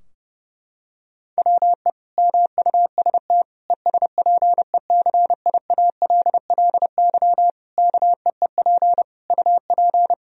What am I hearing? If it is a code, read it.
Morse code, heard as a keyed tone, about 30 wpm, 700 Hz. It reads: WE MUST ESPECIALLY KEEP UP
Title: Sentences from Top 1000 Words - Review Part 4 - 30wpm